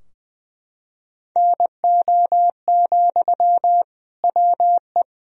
Morse code, heard as a keyed tone, about 20 wpm, 700 Hz. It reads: NO, WE
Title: Sentences from Top 1000 Words - Review Part 3 - 20wpm